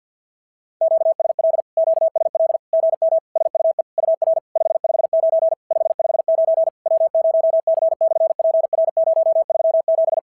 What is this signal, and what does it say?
Morse code, heard as a keyed tone, about 46 wpm, 660 Hz. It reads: QSL QSL GM SUE UR 559 559 W0C/PR037
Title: SOTA QSO Practice 50wpm